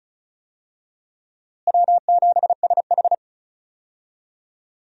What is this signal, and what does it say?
Morse code, heard as a keyed tone, about 36 wpm, 700 Hz. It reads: W7SH
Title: USA Call Signs - 4 Characters - Repeated 3x - 35wpm